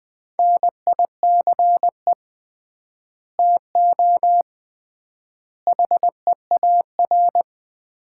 Morse code, heard as a keyed tone, about 20 wpm, 700 Hz. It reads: NICE TO HEAR
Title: Common QSO Phrases - 3x Word-Spacing - 20wpm